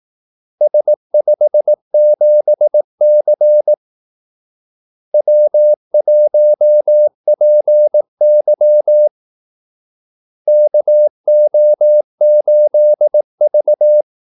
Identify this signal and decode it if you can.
Morse code, heard as a keyed tone, about 18 wpm, 600 Hz. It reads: S57C W1PY KO8V
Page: Kaldesignaler 18 wpm